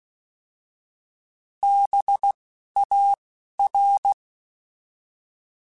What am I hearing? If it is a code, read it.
Morse code, heard as a keyed tone, about 16 wpm, spaced as for 11 wpm, 800 Hz. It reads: BAR
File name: Alphabetical / Alphabetical mp3s / Bs @10